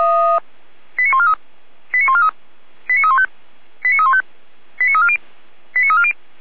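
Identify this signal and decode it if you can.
Five-tone selective call (ZVEI), heard as quick runs of five tones, tones of about 70 ms. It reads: 89143 89143 89315 89315 89340 89340
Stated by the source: ZVEI five-tone selective call received on an SDR.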